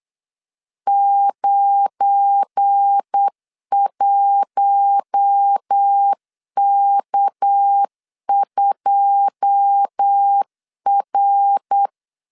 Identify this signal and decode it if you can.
Morse code, heard as a keyed tone, about 8 wpm, 800 Hz. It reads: 91K2R
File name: LEZIONI / 10 SIGNS / 10D COMA 8wpm 3749